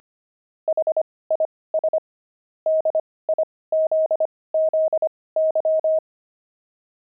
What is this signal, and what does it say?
Morse code, heard as a keyed tone, about 25 wpm, spaced as for 18 wpm, 630 Hz. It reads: HIS DIZZY
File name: pangram-fs-18-cs-25-gloved